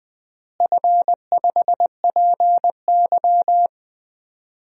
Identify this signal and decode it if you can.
Morse code, heard as a keyed tone, about 20 wpm, 700 Hz. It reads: F5PY
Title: DXCC-100 Call Signs - 4 Characters - Repeated 2x - 20wpm